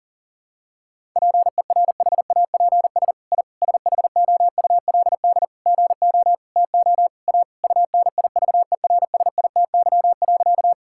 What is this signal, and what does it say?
Morse code, heard as a keyed tone, about 37 wpm, 700 Hz. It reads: PERHAPS I SHOULD GO TO A UNIVERSITY.
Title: Sentences from Top 1000 Words - Review Part 5 - 40wpm